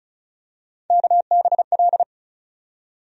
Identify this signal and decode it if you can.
Morse code, heard as a keyed tone, about 36 wpm, 700 Hz. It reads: KBL
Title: USA Call Sign Postfixes - 3 Letters 35wpm